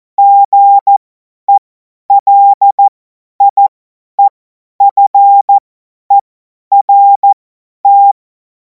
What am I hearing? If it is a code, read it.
Morse code, heard as a keyed tone, about 14 wpm, spaced as for 10 wpm, 800 Hz. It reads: GELIEFERT
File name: Morsekurs / Witze / Witz 23. Juni 2026 50 F70